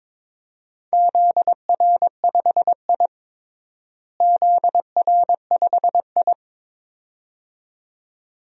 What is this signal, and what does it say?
Morse code, heard as a keyed tone, about 22 wpm, 700 Hz. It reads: ZR5I ZR5I
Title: DXCC-100 Call Signs - 4 Characters - Repeated 2x - 22wpm